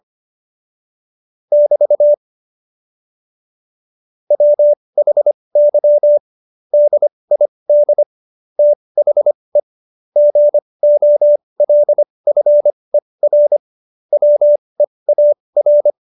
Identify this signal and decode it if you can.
Morse code, heard as a keyed tone, about 25 wpm, spaced as for 20 wpm, 590 Hz. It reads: <BT> WHY DID THE GOLFER WEAR
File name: Jokes-25-char-20-eff